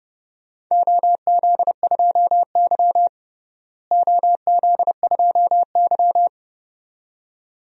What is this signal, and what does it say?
Morse code, heard as a keyed tone, about 30 wpm, 700 Hz. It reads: OZ2Y OZ2Y
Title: DXCC-100 Call Signs - 4 Characters - Repeated 2x - 30wpm